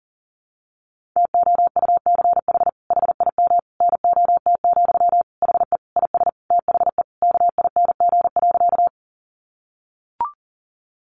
Morse code, heard as a keyed tone, about 40 wpm, 700 Hz. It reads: TOUCH HIM NOT, HE IS THE KING.
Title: Sentences from Top 1200 Words 40wpm